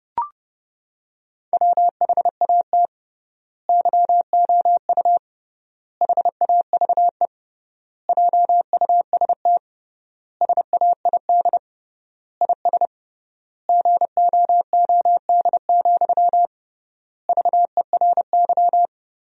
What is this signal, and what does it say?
Morse code, heard as a keyed tone, about 30 wpm, 700 Hz. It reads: WHAT YOU HAVE JUST SAID IS GOOD, VERY
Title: 3x Word-Spacing - Sentences from Top 200 Words - 30wpm